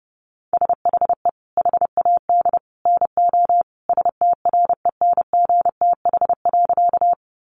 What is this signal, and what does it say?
Morse code, heard as a keyed tone, about 30 wpm, 700 Hz. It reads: SHE HAD NO STRENGTH.
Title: Sentences from Top 1300 Words 30wpm